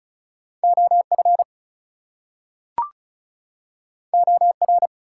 Morse code, heard as a keyed tone, about 36 wpm, 700 Hz. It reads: OF OR
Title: Two Letter Words - 35wpm